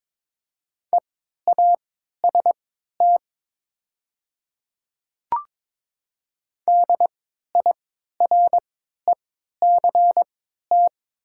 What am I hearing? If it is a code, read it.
Morse code, heard as a keyed tone, about 22 wpm, spaced as for 12 wpm, 690 Hz. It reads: EAST DIRECT
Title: 3x Character Spacing - Top 500 Words - 22wpm